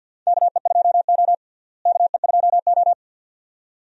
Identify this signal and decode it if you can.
Morse code, heard as a keyed tone, about 51 wpm, 700 Hz. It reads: KE2O KE2O
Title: USA Call Signs - 4 Characters - Repeated 2x - 50wpm